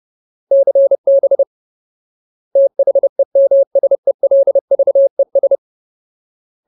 Morse code, heard as a keyed tone, about 30 wpm, 550 Hz. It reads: CB THEMSELVES
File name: Ultimate Word List / 30WPM Ultimate